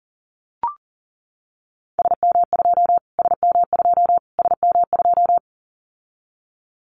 Morse code, heard as a keyed tone, about 40 wpm, 700 Hz. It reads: SM2 SM2 SM2
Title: Callsign Prefixes for Top 25 DXCC - Repeated 3x - 40wpm